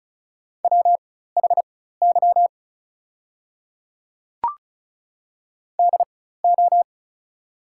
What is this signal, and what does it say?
Morse code, heard as a keyed tone, about 35 wpm, spaced as for 16 wpm, 700 Hz. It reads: WHY DO
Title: 4x Character Spacing - Top 300 Words - 35wpm